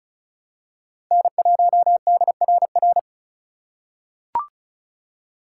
Morse code, heard as a keyed tone, about 36 wpm, 700 Hz. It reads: N1DRR
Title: USA Call Signs - 3x Character Spacing - 35wpm